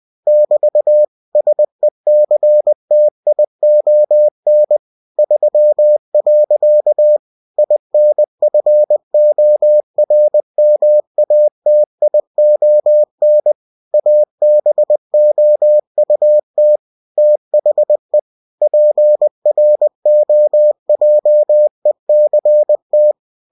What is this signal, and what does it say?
Morse code, heard as a keyed tone, about 20 wpm, 600 Hz. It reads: <BT> SECTION 3. INFORMATION ABOUT THE PROJECT
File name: cw-books / winnie-the-pooh / winnie-20wpm-page-0120